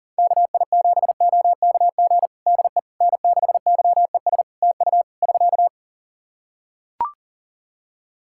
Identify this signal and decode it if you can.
Morse code, heard as a keyed tone, about 40 wpm, 700 Hz. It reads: KI7OKG DE N6YES TU <SK>
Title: Practice QSO - Exchange 4 - Ending - 40wpm